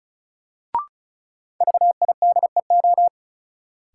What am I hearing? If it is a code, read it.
Morse code, heard as a keyed tone, about 36 wpm, 700 Hz. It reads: VIDEO
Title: Sentences from Top 1000 Words - Review Part 4 - New Word Drill - 35wpm